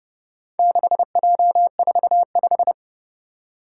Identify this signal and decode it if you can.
Morse code, heard as a keyed tone, about 30 wpm, 700 Hz. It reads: 6J45
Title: Four Challenging Letter-Number Combinations 30wpm